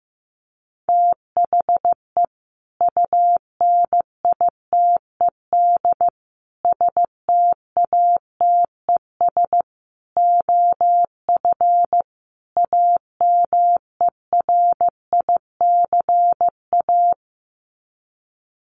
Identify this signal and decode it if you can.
Morse code, heard as a keyed tone, about 15 wpm, 700 Hz. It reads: THE UNITED STATES OF AMERICA
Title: Sets of 5 Words 15wpm